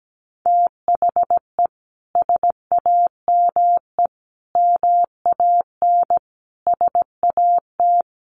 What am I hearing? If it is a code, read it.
Morse code, heard as a keyed tone, about 17 wpm, 700 Hz. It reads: THE SAME MAN SAT
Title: A Sound Of Thunder 17wpm